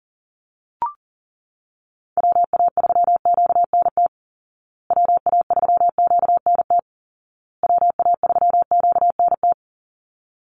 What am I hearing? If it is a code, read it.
Morse code, heard as a keyed tone, about 40 wpm, 700 Hz. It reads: WA3QNT WA3QNT WA3QNT
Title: Call Signs Repeated 2x - 40wpm